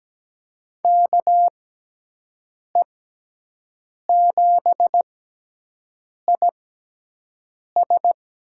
Morse code, heard as a keyed tone, about 17 wpm, spaced as for 6 wpm, 700 Hz. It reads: KE7IS
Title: USA Call Signs - 6 Characters - 6x Character Spacing - 17wpm